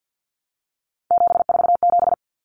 Morse code, heard as a keyed tone, about 50 wpm, 700 Hz. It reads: 747
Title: Common Cut Numbers - 3-Digit 50wpm